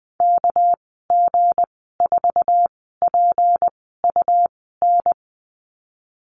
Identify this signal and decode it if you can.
Morse code, heard as a keyed tone, about 20 wpm, spaced as for 14 wpm, 700 Hz. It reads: KG4PUN
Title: 219 Lesson 35g - Number 0 - Call Signs 20wpm